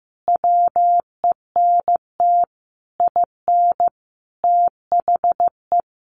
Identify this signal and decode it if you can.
Morse code, heard as a keyed tone, about 15 wpm, 700 Hz. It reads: WENT IN THE
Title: Sentences from Top 1300 Words 15wpm